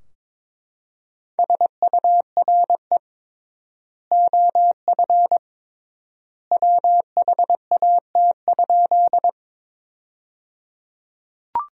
Morse code, heard as a keyed tone, about 22 wpm, 700 Hz. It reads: SURE OF WHAT?
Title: 3x Word-Spacing - Sentences from Top 300 Words - 22wpm